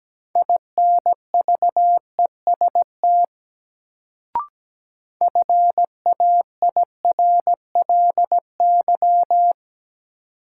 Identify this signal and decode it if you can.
Morse code, heard as a keyed tone, about 17 wpm, 700 Hz. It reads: INVEST FAIRLY
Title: Sentences from Top 1100 Words - New Word Drill Alt - 17wpm